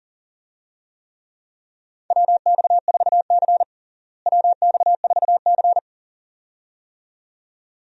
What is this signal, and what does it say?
Morse code, heard as a keyed tone, about 40 wpm, 700 Hz. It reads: WX4C WX4C
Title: USA Call Signs - 4 Characters - Repeated 2x - 40wpm